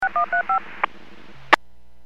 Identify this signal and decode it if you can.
DTMF, heard as telephone touch tones, tones of about 110 ms. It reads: 6135